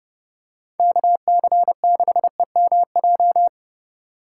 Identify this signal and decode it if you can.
Morse code, heard as a keyed tone, about 30 wpm, 700 Hz. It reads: KC6EMJ